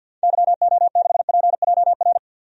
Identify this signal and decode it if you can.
Morse code, heard as a keyed tone, about 52 wpm, 700 Hz. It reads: YO6PJR
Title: DXCC-100 Call Signs - 6 Characters - 50wpm